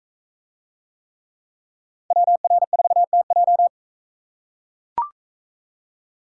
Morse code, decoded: WR4TJ

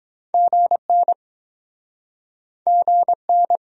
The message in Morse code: GN GN

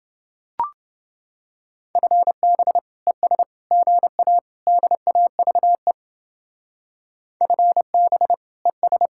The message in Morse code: FB ES GA DAVE FB ES

